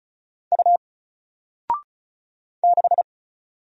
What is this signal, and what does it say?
Morse code, heard as a keyed tone, about 36 wpm, 700 Hz. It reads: U 6
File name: Single-Letter-Number-35wpm